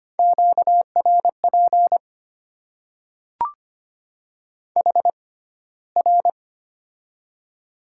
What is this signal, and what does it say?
Morse code, heard as a keyed tone, about 25 wpm, spaced as for 15 wpm, 700 Hz. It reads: QRP HR
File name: Top 90 QSO Elements - 6x Character Spacing - 25wpm